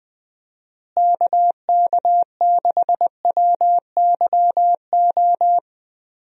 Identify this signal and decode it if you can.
Morse code, heard as a keyed tone, about 20 wpm, 700 Hz. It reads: KK6WYO
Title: USA Call Signs - 6 Characters - Repeated 2x - 20wpm